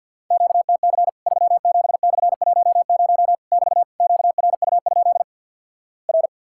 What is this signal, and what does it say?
Morse code, heard as a keyed tone, about 50 wpm, 700 Hz. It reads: QTC 37/10 <BT> QRU? R
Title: Worked All Europe DX Contest - QTC Exchange - 50wpm